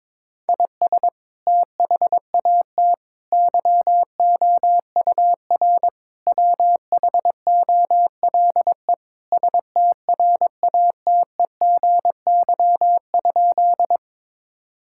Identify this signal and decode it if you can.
Morse code, heard as a keyed tone, about 21 wpm, 700 Hz. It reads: IS THAT YOUR WHOLE STRATEGY?
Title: Sentences from Top 1000 Words - Review Part 5 - 22wpm